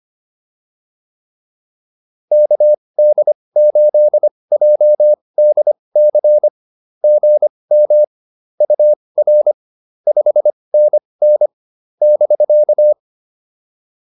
Morse code, heard as a keyed tone, about 25 wpm, spaced as for 20 wpm, 590 Hz. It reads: KD8JDC GM UR 5NN <BK>